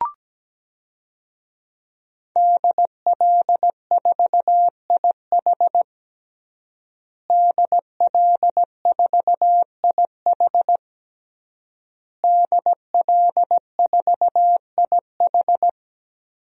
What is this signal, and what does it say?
Morse code, heard as a keyed tone, about 17 wpm, 700 Hz. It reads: DL4IH DL4IH DL4IH
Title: DXCC-100 Call Signs - 5 Characters - Repeated 3x - 17wpm